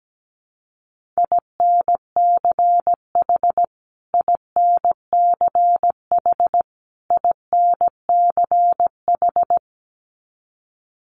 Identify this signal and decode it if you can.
Morse code, heard as a keyed tone, about 17 wpm, 700 Hz. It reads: INCH INCH INCH
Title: Top 500 Words Repeated 17wpm